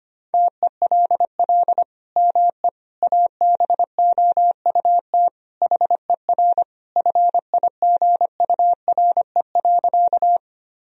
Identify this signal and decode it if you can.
Morse code, heard as a keyed tone, about 25 wpm, 700 Hz. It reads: TELL ME ABOUT HER FIGURE.
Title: Sentences from Top 1000 Words - Review Part 2 - 25wpm